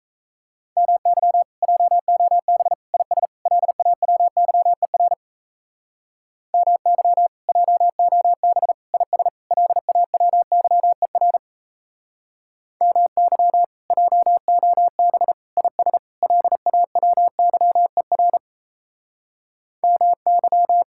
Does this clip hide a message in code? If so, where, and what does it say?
Yes: Morse code from 0.8 to 20.8 s, heard as a keyed tone, about 36 wpm, 700 Hz. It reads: MY JOB IS LAWYER MY JOB IS LAWYER MY JOB IS LAWYER MY